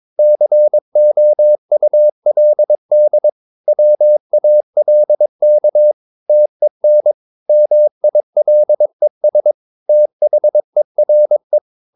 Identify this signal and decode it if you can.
Morse code, heard as a keyed tone, about 22 wpm, 600 Hz. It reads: COULD WALK TEN MILES THERE
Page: Romanen Never af Ken Follet på engelsk.